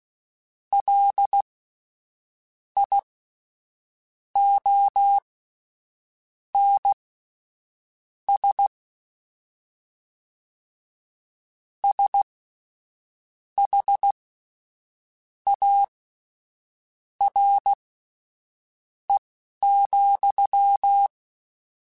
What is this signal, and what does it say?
Morse code, heard as a keyed tone, about 16 wpm, spaced as for 6 wpm, 800 Hz. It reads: LIONS SHARE,
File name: Misc / Misc-mp3s / WordPairs sp @10